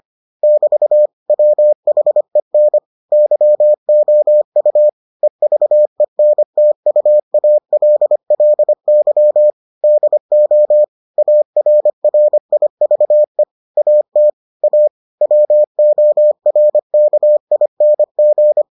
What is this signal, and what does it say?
Morse code, heard as a keyed tone, about 25 wpm, 600 Hz. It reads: <BT> WHEN YOU EVENTUALLY DO ARRIVE AT A WORKING